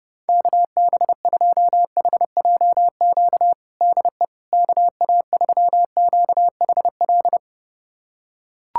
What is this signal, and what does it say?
Morse code, heard as a keyed tone, about 30 wpm, 700 Hz. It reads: KB2HJQ DE KA3QHL
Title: USA Callsign DE Callsign 30wpm